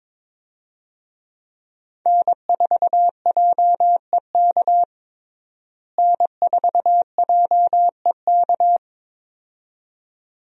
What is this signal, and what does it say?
Morse code, heard as a keyed tone, about 22 wpm, 700 Hz. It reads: N4JEK N4JEK
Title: USA Call Signs - 5 Characters - Repeated 2x - 22wpm